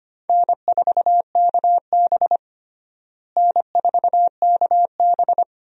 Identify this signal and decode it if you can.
Morse code, heard as a keyed tone, about 25 wpm, 700 Hz. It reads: N4KB N4KB
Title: USA Call Signs - 4 Characters - Repeated 2x - 25wpm